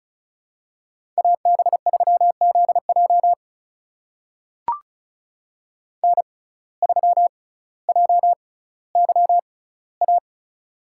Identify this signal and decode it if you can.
Morse code, heard as a keyed tone, about 36 wpm, spaced as for 18 wpm, 700 Hz. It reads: AB3ZJ N3JYA
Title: USA Call Signs - 5 Characters - 6x Character Spacing - 35wpm